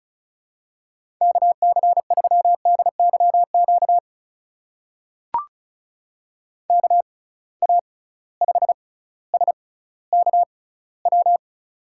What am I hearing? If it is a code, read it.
Morse code, heard as a keyed tone, about 36 wpm, spaced as for 18 wpm, 700 Hz. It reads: KC3DYQ KA5SKW
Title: USA Call Signs - 6 Characters - 6x Character Spacing - 35wpm